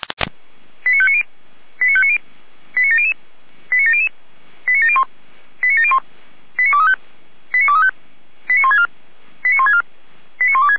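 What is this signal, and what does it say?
Five-tone selective call (ZVEI), heard as quick runs of five tones, tones of about 70 ms. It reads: 89590 89590 89700 89700 89701 89701 89235 89235 89164 89164 89126
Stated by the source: ZVEI five-tone selective call 89120, received by SDR.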